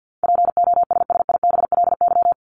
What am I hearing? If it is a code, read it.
Morse code, heard as a keyed tone, about 50 wpm, 700 Hz. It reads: POSSIBLY
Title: LY Suffix - Top 100 Words with Base Words 50wpm